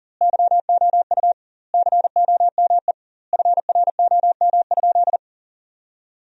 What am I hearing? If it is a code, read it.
Morse code, heard as a keyed tone, about 40 wpm, 700 Hz. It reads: YOU COME FROM?
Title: 2x Word-Spacing - Sentences from Top 200 Words - 40wpm